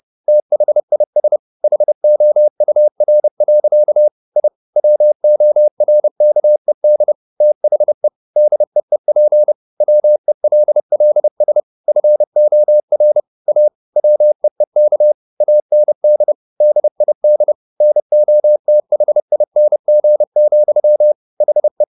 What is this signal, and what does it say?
Morse code, heard as a keyed tone, about 30 wpm, 600 Hz. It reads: THIS HOUR. I WORKED THE DEEP WELLS FOR A WEEK AND DID NOTHING, HE
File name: the old man and the sea / the old man and the sea 1 30wpm